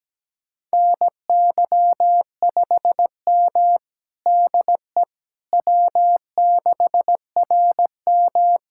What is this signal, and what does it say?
Morse code, heard as a keyed tone, about 17 wpm, 700 Hz. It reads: NY5M DE W6RM
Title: USA Callsign DE Callsign - 4 Characters - 17wpm